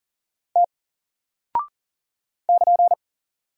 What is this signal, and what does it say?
Morse code, heard as a keyed tone, about 40 wpm, 700 Hz. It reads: T <KN>
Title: Single Character (Letter:Number:Punctuation:Prosign) 40wpm